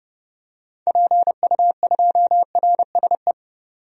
Morse code, heard as a keyed tone, about 30 wpm, 700 Hz. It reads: PU2RSE
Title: DXCC-100 Call Signs - 6 Characters - 30wpm